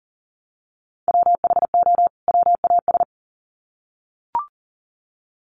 Morse code, heard as a keyed tone, about 40 wpm, 700 Hz. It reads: WHO WAS
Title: Sets of 2 Words 40wpm